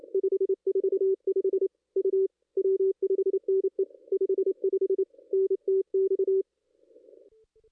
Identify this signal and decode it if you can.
Morse code, heard as a keyed tone, about 28 wpm, 380 Hz. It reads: 545 U W5NE 55 NTX
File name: contests / ss-cw-k5tr-op-n5rz -0400